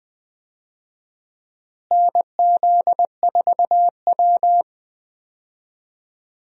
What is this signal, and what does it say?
Morse code, heard as a keyed tone, about 21 wpm, 700 Hz. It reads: NZ4W
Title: USA Call Signs - 4 Characters - Repeated 3x - 20wpm